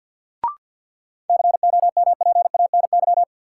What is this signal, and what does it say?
Morse code, heard as a keyed tone, about 53 wpm, 700 Hz. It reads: COMPANY
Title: Sentences from Top 600 Words - New Word Drill - 50wpm